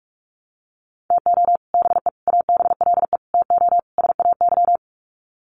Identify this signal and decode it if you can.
Morse code, heard as a keyed tone, about 44 wpm, 700 Hz. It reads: TO BE ABLE TO SAY
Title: Sets of 5 Words Encore2 45wpm